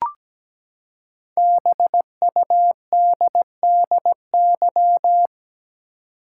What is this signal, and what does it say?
Morse code, heard as a keyed tone, about 17 wpm, 700 Hz. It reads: BUDDY
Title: Five Letter Words - 17wpm